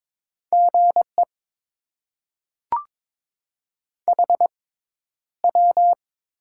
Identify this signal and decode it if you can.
Morse code, heard as a keyed tone, about 22 wpm, 700 Hz. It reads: G E H W
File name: Top 20 QSO Elements - 6x Character Spacing - 22wpm